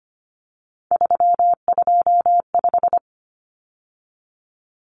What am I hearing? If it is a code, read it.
Morse code, heard as a keyed tone, about 25 wpm, 700 Hz. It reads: 325
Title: Common Cut Numbers - 3-Digit 25wpm